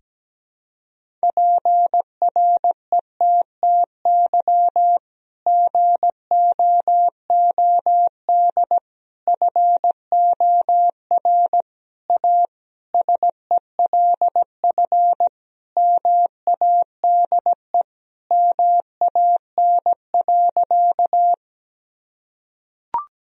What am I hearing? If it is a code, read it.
Morse code, heard as a keyed tone, about 17 wpm, 700 Hz. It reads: PRETTY GOOD FOR A SELF MADE MAN.
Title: Sentences from Top 1000 Words - Review Part 1 - 17wpm